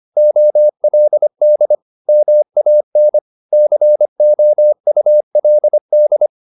Morse code, heard as a keyed tone, about 25 wpm, 600 Hz. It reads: OLD MAN COULD
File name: the old man and the sea part 1 25wpm